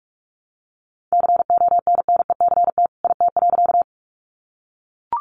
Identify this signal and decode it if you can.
Morse code, heard as a keyed tone, about 44 wpm, 700 Hz. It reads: CONNECT IT.
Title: Sentences from Top 1500 Words 45wpm